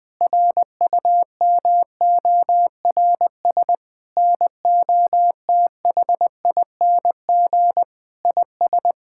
Morse code, heard as a keyed tone, about 20 wpm, 700 Hz. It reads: RUMORS NOTHING IS
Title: Sentences with 8-Word Phrases - 20wpm